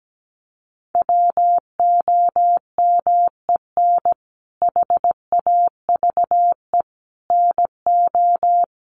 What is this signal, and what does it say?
Morse code, heard as a keyed tone, about 17 wpm, 700 Hz. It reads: WOMEN HAVE NO
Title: Sentences from Top 1400 Words 17wpm